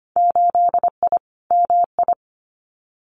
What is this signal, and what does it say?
Morse code, heard as a keyed tone, about 25 wpm, 700 Hz. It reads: 8I MI
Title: Winter Field Day Exchanges Encore 25wpm